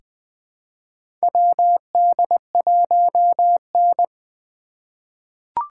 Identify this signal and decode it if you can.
Morse code, heard as a keyed tone, about 20 wpm, 700 Hz. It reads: WD1N
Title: USA Call Signs - 2x Character Spacing - 20wpm